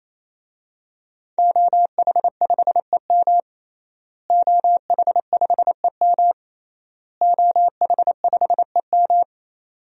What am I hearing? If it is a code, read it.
Morse code, heard as a keyed tone, about 28 wpm, 700 Hz. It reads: OH5EM OH5EM OH5EM
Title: DXCC-100 Call Signs - 5 Characters - Repeated 3x - 28wpm